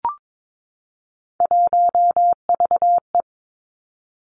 Morse code, heard as a keyed tone, about 22 wpm, 700 Hz. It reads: 1VE
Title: Three Challenging Letter-Number Combinations 22wpm